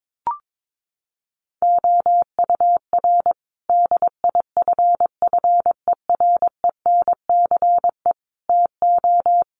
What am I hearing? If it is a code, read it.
Morse code, heard as a keyed tone, about 22 wpm, 700 Hz. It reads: OUR DIFFERENCE TO
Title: Sentences from Top 1500 Words 22wpm